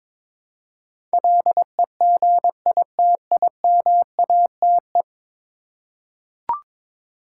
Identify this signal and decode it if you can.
Morse code, heard as a keyed tone, about 22 wpm, 700 Hz. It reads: LEGITIMATE
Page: morse-code-ninja-practice-22wpm